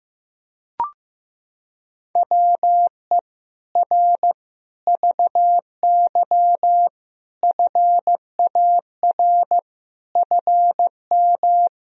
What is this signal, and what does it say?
Morse code, heard as a keyed tone, about 16 wpm, 700 Hz. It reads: WE R VY FAR FM
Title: Sentences from Top 300 Words and Abbreviations 15wpm